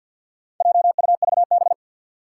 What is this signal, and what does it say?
Morse code, heard as a keyed tone, about 51 wpm, 700 Hz. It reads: JUVB
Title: Four Challenging Letter-Number Combinations 50wpm